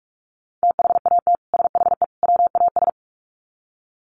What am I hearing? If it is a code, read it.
Morse code, heard as a keyed tone, about 44 wpm, 700 Hz. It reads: THAT SHE WAS
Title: Sets of 3 Words 45wpm